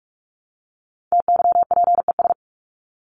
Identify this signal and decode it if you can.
Morse code, heard as a keyed tone, about 44 wpm, 700 Hz. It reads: TYPES
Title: ES:S Suffix - Top 100 Words with Base Words 45wpm